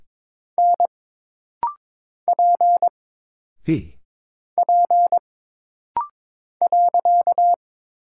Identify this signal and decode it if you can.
Morse code, heard as a keyed tone, about 22 wpm, 700 Hz. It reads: N P P .
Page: This rapid-fire timing is geared to push your ICR (Instant Character Recognition) skills.